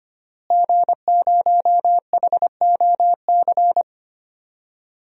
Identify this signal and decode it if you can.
Morse code, heard as a keyed tone, about 25 wpm, 700 Hz. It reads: G0HOC
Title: DXCC-100 Call Signs - 5 Characters - Repeated 3x - 25wpm